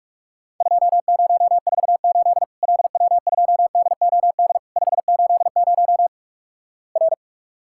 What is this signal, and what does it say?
Morse code, heard as a keyed tone, about 46 wpm, 700 Hz. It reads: 2048 LW2DOD 580 R
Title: Worked All Europe DX Contest - QTC Exchange - 45wpm